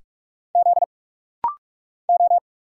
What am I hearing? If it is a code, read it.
Morse code, heard as a keyed tone, about 45 wpm, 700 Hz. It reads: Z O
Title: Single Letter-Number - Rapid-Fire 45wpm